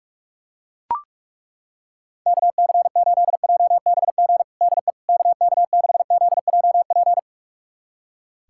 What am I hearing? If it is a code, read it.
Morse code, heard as a keyed tone, about 41 wpm, 700 Hz. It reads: KC8JBG DE KK6ZJP
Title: USA Callsign DE Callsign - 6 Characters - 45wpm